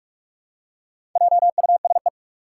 Morse code, heard as a keyed tone, about 44 wpm, 700 Hz. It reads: JUSE
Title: Four Challenging Letter-Number Combinations 45wpm